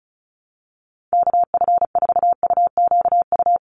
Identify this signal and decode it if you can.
Morse code, heard as a keyed tone, about 34 wpm, 700 Hz. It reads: KF4UQU
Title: USA Call Signs 35wpm